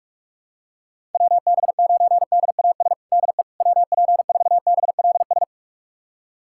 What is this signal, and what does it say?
Morse code, heard as a keyed tone, about 46 wpm, 700 Hz. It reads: WB9DAS DE WP4BLS
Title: USA Callsign DE Callsign - 6 Characters - 45wpm